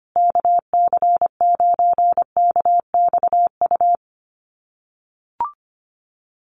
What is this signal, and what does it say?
Morse code, heard as a keyed tone, about 25 wpm, 700 Hz. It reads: KC9KXU
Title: USA Call Signs Repeated 3x 25wpm